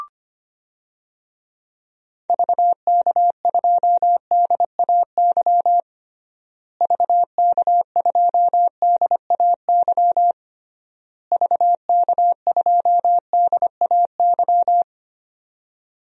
Morse code, heard as a keyed tone, about 25 wpm, 700 Hz. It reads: VK2DAY VK2DAY VK2DAY
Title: DXCC-100 Call Signs - 6 Characters - Repeated 3x - 25wpm